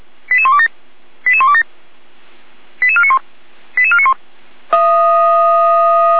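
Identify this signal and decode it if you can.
Five-tone selective call (ZVEI), heard as quick runs of five tones, tones of about 70 ms. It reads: 88127 88127 88491 88491
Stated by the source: ZVEI five-tone call received by SDR, from Feuerwehr Orlamünde fire brigade audio archive.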